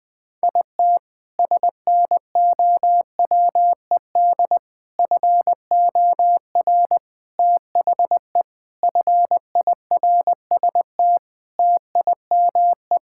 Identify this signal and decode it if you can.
Morse code, heard as a keyed tone, about 20 wpm, 700 Hz. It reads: IT SNOWED FOR THE FIRST TIME
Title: Sentences with 6-Word Phrases - 20wpm